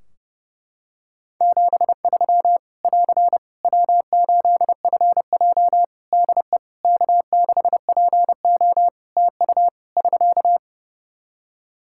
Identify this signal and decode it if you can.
Morse code, heard as a keyed tone, about 30 wpm, 700 Hz. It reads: 73 <AR> W8FJ DE K6PO TU <SK>
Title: Practice QSO - Exchange 4 - Ending - 30wpm